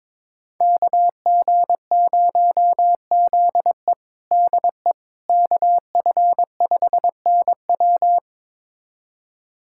Morse code, heard as a keyed tone, about 22 wpm, 700 Hz. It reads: KG0ZE DE KF5NW
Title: USA Callsign DE Callsign - 5 Characters - 22wpm